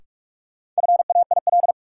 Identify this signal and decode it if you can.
Morse code, heard as a keyed tone, about 46 wpm, 700 Hz. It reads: FAIL
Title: Sentences from Top 1100 Words - New Word Drill - 45wpm